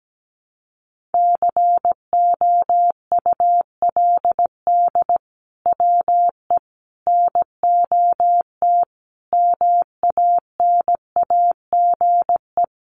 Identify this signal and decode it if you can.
Morse code, heard as a keyed tone, about 17 wpm, 700 Hz. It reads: COULD WE NOT MANAGE
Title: Sentences from Top 1400 Words 17wpm